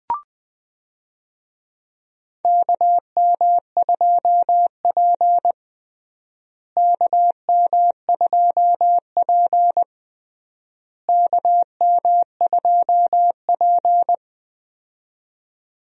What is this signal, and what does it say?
Morse code, heard as a keyed tone, about 21 wpm, 700 Hz. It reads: KM2P KM2P KM2P
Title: USA Call Signs - 4 Characters - Repeated 3x - 20wpm